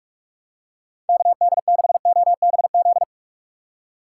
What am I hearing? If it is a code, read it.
Morse code, heard as a keyed tone, about 43 wpm, 700 Hz. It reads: KD6OBZ
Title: USA Call Signs - 6 Characters - 4x Character Spacing - 45wpm